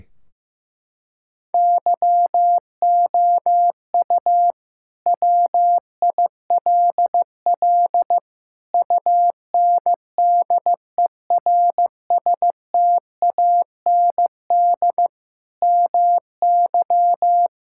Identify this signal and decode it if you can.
Morse code, heard as a keyed tone, about 15 wpm, 700 Hz. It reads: YOU WILL UNDERSTAND MY
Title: Sentences from Top 1600 Words 15wpm